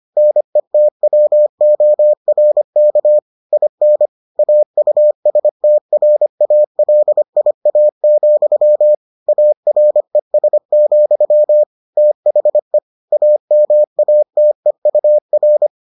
Morse code, heard as a keyed tone, about 25 wpm, 600 Hz. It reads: NETWORK IN AUSTRALIA, ARES, THE AMATEUR